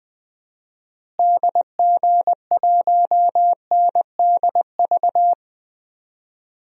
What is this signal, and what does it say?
Morse code, heard as a keyed tone, about 20 wpm, 700 Hz. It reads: DG1NDV
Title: DXCC-100 Call Signs - 6 Characters - Repeated 3x - 20wpm